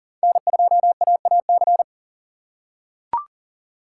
Morse code, heard as a keyed tone, about 40 wpm, 700 Hz. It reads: N2AAC
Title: USA Call Signs - 5 Characters - 3x Character Spacing - 40wpm